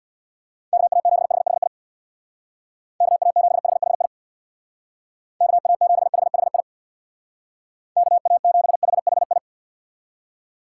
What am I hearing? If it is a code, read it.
Morse code, heard as a keyed tone, about 64 wpm, 700 Hz. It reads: KA7HHI KA7HHI KA7HHI KA7HHI